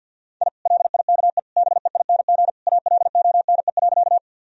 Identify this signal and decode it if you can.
Morse code, heard as a keyed tone, about 51 wpm, 700 Hz. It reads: I LIKE BEING ALONE.
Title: Sentences from Top 1000 Words - Review Part 3 - 50wpm